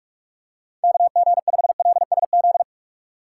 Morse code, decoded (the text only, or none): KG5LSZ